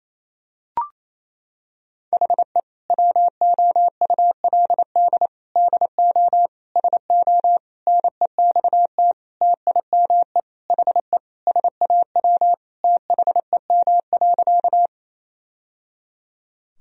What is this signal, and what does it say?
Morse code, heard as a keyed tone, about 28 wpm, 700 Hz. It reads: HE WOULD DO SO NEXT TIME HE SAW THEM.
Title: Sentences from Top 1000 Words - Review Part 1 - 28wpm